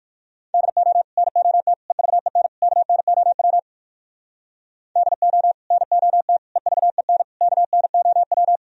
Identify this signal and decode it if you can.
Morse code, heard as a keyed tone, about 57 wpm, 700 Hz. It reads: DO NOT EVEN KNOW DO NOT EVEN KNOW